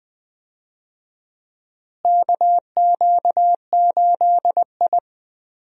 Morse code, heard as a keyed tone, about 21 wpm, 700 Hz. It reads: KQ8I
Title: USA Call Signs - 4 Characters - Repeated 3x - 20wpm